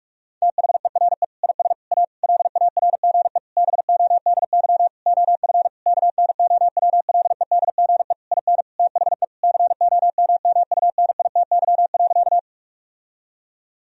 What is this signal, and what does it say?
Morse code, heard as a keyed tone, about 41 wpm, 700 Hz. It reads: THERE IS A LARGE BODY OF KNOWLEDGE IN THE COMMUNITY.
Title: Sentences from Top 1000 Words - Review Part 4 - 45wpm